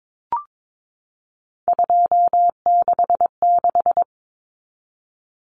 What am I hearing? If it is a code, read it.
Morse code, heard as a keyed tone, about 22 wpm, 700 Hz. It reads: 266
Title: Common Cut Numbers - 3-Digit 22wpm